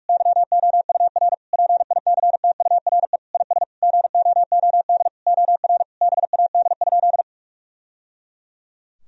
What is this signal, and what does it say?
Morse code, heard as a keyed tone, about 41 wpm, 700 Hz. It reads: YOUR PICTURE IS GOOD OR BAD?
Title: Sentences from Top 1000 Words - Review Part 1 - 45wpm